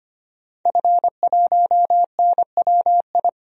Morse code, heard as a keyed tone, about 25 wpm, 700 Hz. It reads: F1NWI
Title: DXCC-100 Call Signs - 5 Characters - 25wpm